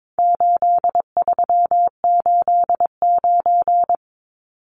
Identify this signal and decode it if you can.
Morse code, heard as a keyed tone, about 22 wpm, 700 Hz. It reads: 8389
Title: Numbers - 4-Digit 22wpm